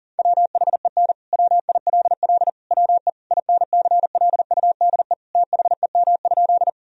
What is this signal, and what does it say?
Morse code, heard as a keyed tone, about 40 wpm, 700 Hz. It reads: WHEN WILL WE INCLUDE THEM?
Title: Sentences from Top 1000 Words - Review Part 3 - 40wpm